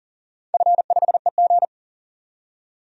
Morse code, heard as a keyed tone, about 40 wpm, 700 Hz. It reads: F5EG
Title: DXCC-100 Call Signs - 4 Characters - 40wpm